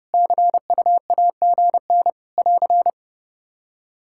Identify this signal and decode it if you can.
Morse code, heard as a keyed tone, about 32 wpm, 700 Hz. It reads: CUAGN <AR>
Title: Practice QSO - Exchange 4 - Ending - 30wpm